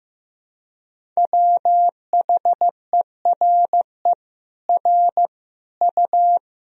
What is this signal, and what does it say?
Morse code, heard as a keyed tone, about 15 wpm, 700 Hz. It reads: WHERE R U
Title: Sentences from Top 200 Words and Abbreviations 15wpm